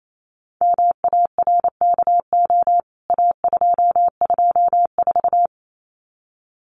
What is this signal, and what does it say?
Morse code, heard as a keyed tone, about 28 wpm, 700 Hz. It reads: MARKO A224
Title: CWT Exchange 28wpm